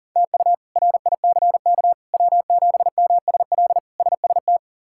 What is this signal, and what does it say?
Morse code, heard as a keyed tone, about 40 wpm, 700 Hz. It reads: TU RICK W7MSL SST
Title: K1USN SST QSO Practice - 40wpm